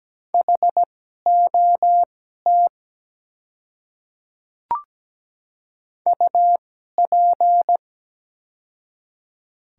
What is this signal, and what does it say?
Morse code, heard as a keyed tone, about 17 wpm, spaced as for 12 wpm, 700 Hz. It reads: HOT UP
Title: 2x Character Spacing - Top 100 Words - 17wpm